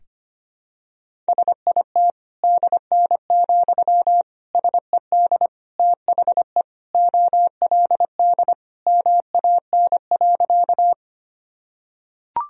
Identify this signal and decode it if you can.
Morse code, heard as a keyed tone, about 25 wpm, 700 Hz. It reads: SIT DN, SED THE OLD MAN.
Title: Sentences from Top 500 Words and Abbreviations 25wpm